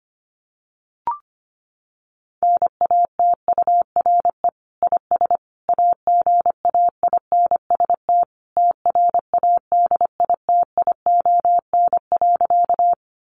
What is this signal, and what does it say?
Morse code, heard as a keyed tone, about 25 wpm, 700 Hz. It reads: NATURE IS AGAINST TRADITION.
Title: Sentences from Top 1500 Words 25wpm